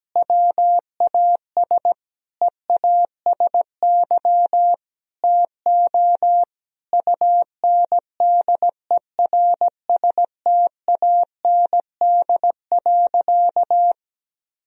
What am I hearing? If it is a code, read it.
Morse code, heard as a keyed tone, about 17 wpm, 700 Hz. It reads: WAS EASY TO UNDERSTAND.
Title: Sentences from Top 1000 Words - Review Part 3 - 17wpm